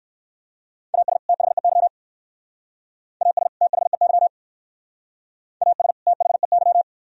Morse code, heard as a keyed tone, about 55 wpm, 700 Hz. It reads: AS THEY AS THEY AS THEY
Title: Speed-Racing - Sets of 2 Words 45wpm